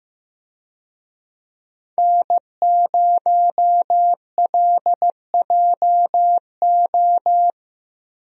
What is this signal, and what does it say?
Morse code, heard as a keyed tone, about 15 wpm, 700 Hz. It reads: N0LJO
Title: USA Call Signs - 5 Characters - Repeated 3x - 15wpm